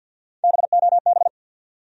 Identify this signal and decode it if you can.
Morse code, heard as a keyed tone, about 48 wpm, 700 Hz. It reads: BOB
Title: USA Common Names - Max 3 Letters - 50wpm